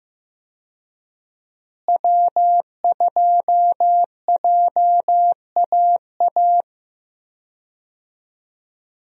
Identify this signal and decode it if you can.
Morse code, heard as a keyed tone, about 15 wpm, 700 Hz. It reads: W2JAA
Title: USA Call Signs - 5 Characters - Repeated 2x - 15wpm